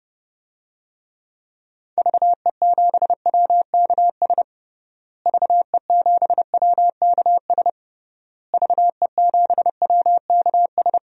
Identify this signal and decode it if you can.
Morse code, heard as a keyed tone, about 30 wpm, 700 Hz. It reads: VE7WKS VE7WKS VE7WKS
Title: DXCC-100 Call Signs - 6 Characters - Repeated 3x - 30wpm